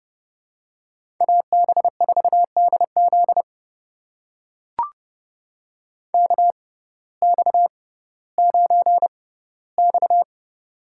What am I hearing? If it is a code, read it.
Morse code, heard as a keyed tone, about 32 wpm, spaced as for 17 wpm, 700 Hz. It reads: AB4DZ KX9X
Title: USA Call Signs - 6x Character Spacing - 30wpm